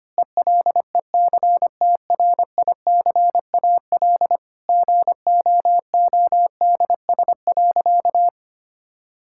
Morse code, heard as a keyed tone, about 25 wpm, 700 Hz. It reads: ELECTRICAL GOODS.
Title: Sentences from Top 1000 Words - Review Part 6 - 25wpm